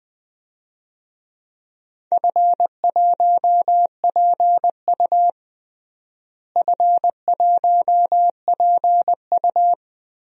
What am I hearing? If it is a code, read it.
Morse code, heard as a keyed tone, about 20 wpm, 700 Hz. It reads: F1PU F1PU
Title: DXCC-100 Call Signs - 4 Characters - Repeated 3x - 20wpm